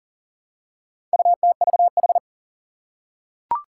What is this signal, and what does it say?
Morse code, heard as a keyed tone, about 40 wpm, 700 Hz. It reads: UTVH
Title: Four Challenging Letter-Number Combinations 40wpm